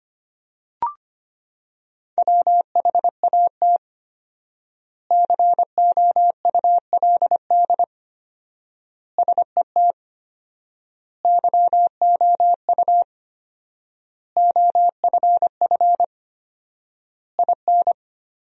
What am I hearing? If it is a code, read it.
Morse code, heard as a keyed tone, about 25 wpm, 700 Hz. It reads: WHAT COULD SET YOU OFF IN
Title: 4x Word-Spacing - Sentences from Top 200 Words - 25wpm